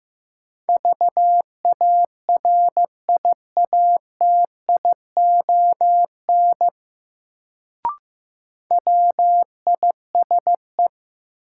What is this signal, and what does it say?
Morse code, heard as a keyed tone, about 15 wpm, 700 Hz. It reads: VARIATION WISE
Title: Sentences from Top 1600 Words - New Word Drill Alt - 15wpm